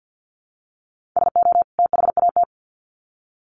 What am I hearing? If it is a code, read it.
Morse code, heard as a keyed tone, about 50 wpm, 700 Hz. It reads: SO THAT
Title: Sets of 2 Words 50wpm